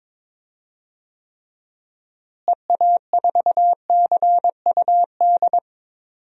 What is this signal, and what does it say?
Morse code, heard as a keyed tone, about 21 wpm, 700 Hz. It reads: EA4CUD
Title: DXCC-100 Call Signs - 6 Characters - Repeated 2x - 22wpm